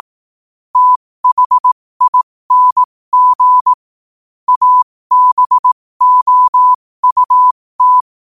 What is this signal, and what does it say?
Morse code, heard as a keyed tone, about 18 wpm, spaced as for 15 wpm, 1000 Hz. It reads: THING ABOUT
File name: 15wpm